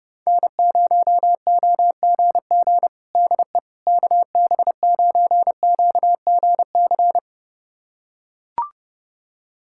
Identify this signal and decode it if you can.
Morse code, heard as a keyed tone, about 30 wpm, 700 Hz. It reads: N0OGG DE KB9QGC
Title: USA Callsign DE Callsign 30wpm